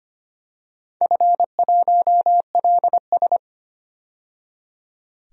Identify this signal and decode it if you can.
Morse code, heard as a keyed tone, about 25 wpm, 700 Hz. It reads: F1LS
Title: DXCC-100 Call Signs - 4 Characters - 25wpm